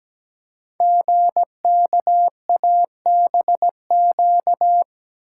Morse code, heard as a keyed tone, about 17 wpm, 700 Hz. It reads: GKABQ
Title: Code Groups - 5 Letters - 17wpm